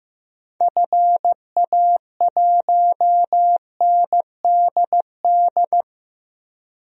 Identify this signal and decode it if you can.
Morse code, heard as a keyed tone, about 15 wpm, 700 Hz. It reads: FA1NDD
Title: DXCC-100 Call Signs - 6 Characters - 15wpm